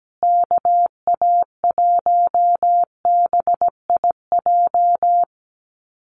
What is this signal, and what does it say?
Morse code, heard as a keyed tone, about 17 wpm, 700 Hz. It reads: KA1BIJ
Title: USA Call Signs Repeated 2x 17wpm